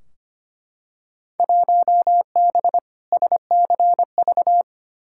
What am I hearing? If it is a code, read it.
Morse code, heard as a keyed tone, about 25 wpm, 700 Hz. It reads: 1B SCV
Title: Field Day Exchanges - 25wpm